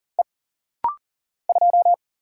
Morse code, heard as a keyed tone, about 40 wpm, 700 Hz. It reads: E 2
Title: Single Letter-Number - Rapid-Fire 40wpm